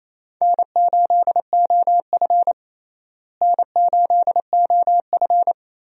Morse code, heard as a keyed tone, about 28 wpm, 700 Hz. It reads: N8OF N8OF
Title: USA Call Signs - 4 Characters - Repeated 3x - 28wpm